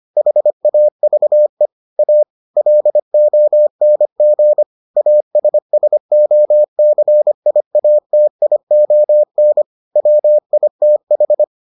Morse code, heard as a keyed tone, about 25 wpm, 600 Hz. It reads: HAVE A LONG ASSOCIATION WITH